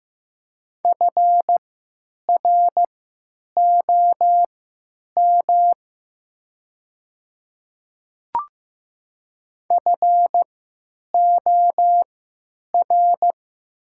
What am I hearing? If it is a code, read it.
Morse code, heard as a keyed tone, about 15 wpm, spaced as for 9 wpm, 700 Hz. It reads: FROM FOR
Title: 3x Character Spacing - Top 100 Words - 15wpm